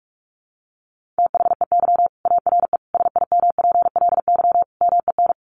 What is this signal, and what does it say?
Morse code, heard as a keyed tone, about 44 wpm, 700 Hz. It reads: THEY ARE SIMPLY MEN
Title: Sentences from Top 1500 Words 45wpm